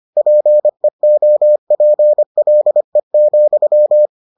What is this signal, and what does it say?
Morse code, heard as a keyed tone, about 25 wpm, 600 Hz. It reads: PEOPLE,